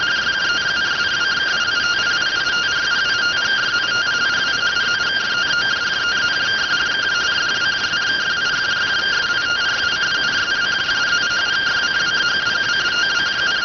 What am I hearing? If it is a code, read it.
Two-tone teleprinter signal, about 170 Hz shift.